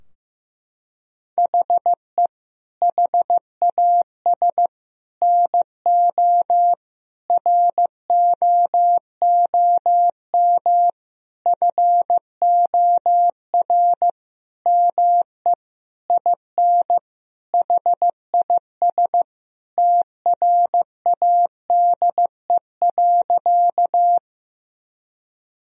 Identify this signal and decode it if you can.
Morse code, heard as a keyed tone, about 15 wpm, 700 Hz. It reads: HE HAS NO ROOM FOR ME IN HIS TRADE.
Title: Sentences from Top 1000 Words - Review Part 4 - 15wpm